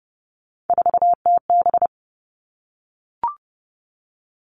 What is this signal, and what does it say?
Morse code, heard as a keyed tone, about 30 wpm, 700 Hz. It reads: 4TB
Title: Three Challenging Letter-Number Combinations 30wpm